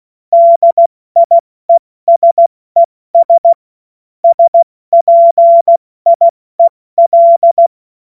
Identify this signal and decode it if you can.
Morse code, heard as a keyed tone, about 16 wpm, spaced as for 14 wpm, 680 Hz. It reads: DIESES SPIEL
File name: Morsekurs / Witze / Witz 25. Mai 2021, 70 F80